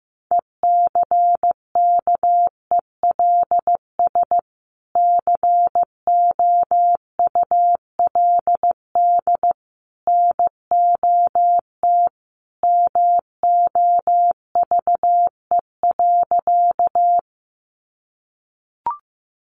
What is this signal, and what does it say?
Morse code, heard as a keyed tone, about 15 wpm, 700 Hz. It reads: ECKELS COULD NOT MOVE.
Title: A Sound Of Thunder 15wpm